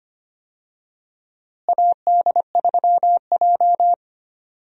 Morse code, heard as a keyed tone, about 25 wpm, 700 Hz. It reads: AD3J
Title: USA Call Signs - 4 Characters - Repeated 3x - 25wpm